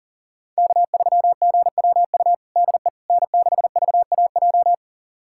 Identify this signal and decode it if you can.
Morse code, heard as a keyed tone, about 40 wpm, 700 Hz. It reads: K3GWU DE N6VAJ
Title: USA Callsign DE Callsign - 5 Characters - 40wpm